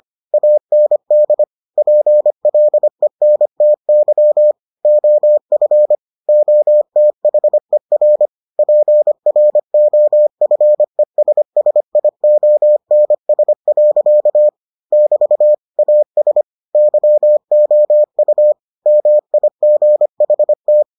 Morse code, heard as a keyed tone, about 25 wpm, 600 Hz. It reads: AND PLENTY OF OTHER PROFESSIONS. <BT> AS YOU MIGHT